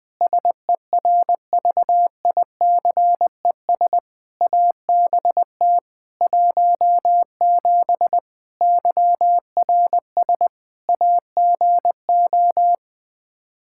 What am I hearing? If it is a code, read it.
Morse code, heard as a keyed tone, about 20 wpm, 700 Hz. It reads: SERVICES ABT 17 YRS AGO
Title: K1USN SST QSO Practice 20wpm